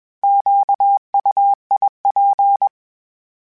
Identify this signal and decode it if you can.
Morse code, heard as a keyed tone, about 21 wpm, 800 Hz. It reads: QUIP
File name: Misc / Misc-mp3s / Misc @20